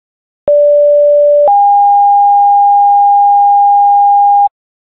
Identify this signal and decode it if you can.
Two-tone page, heard as a one-second tone, then a three-second tone, about 580 then 810 Hz. It is heard from 0.5 to 4.5 s.